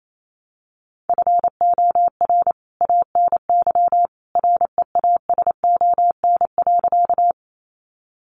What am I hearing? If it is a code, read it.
Morse code, heard as a keyed tone, about 28 wpm, 700 Hz. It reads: FOR ANY REASON.
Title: A Sound Of Thunder 28wpm